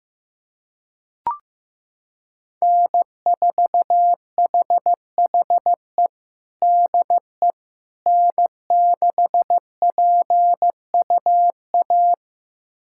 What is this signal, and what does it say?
Morse code, heard as a keyed tone, about 15 wpm, 700 Hz. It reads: N4HHE DE N6PUA
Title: USA Callsign DE Callsign - 5 Characters - 15wpm